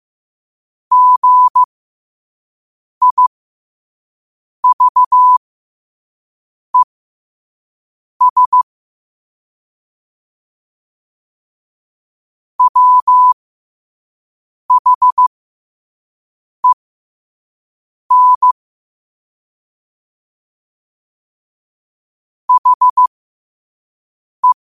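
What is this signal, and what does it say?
Morse code, heard as a keyed tone, about 15 wpm, spaced as for 5 wpm, 1000 Hz. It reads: GIVES WHEN HE
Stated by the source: Morse code practice at 5 wpm.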